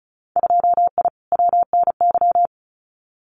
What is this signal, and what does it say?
Morse code, heard as a keyed tone, about 34 wpm, 700 Hz. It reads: 2I WNY
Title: Winter Field Day Exchanges 35wpm